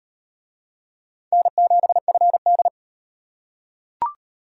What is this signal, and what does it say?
Morse code, heard as a keyed tone, about 37 wpm, 700 Hz. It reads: N7FD